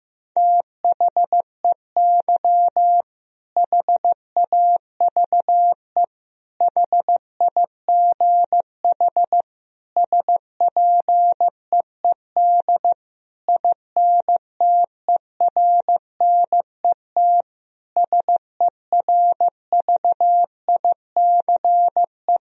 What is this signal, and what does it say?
Morse code, heard as a keyed tone, about 14 wpm, 700 Hz. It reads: THEY HAVE HIGH SPEED INTERNET SERVICE
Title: Sentences from Top 1000 Words - Review Part 4 - 15wpm